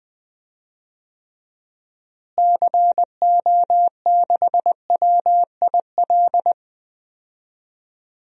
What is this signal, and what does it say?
Morse code, heard as a keyed tone, about 20 wpm, 700 Hz. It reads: CO6WIL
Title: DXCC-100 Call Signs - 6 Characters - Repeated 2x - 20wpm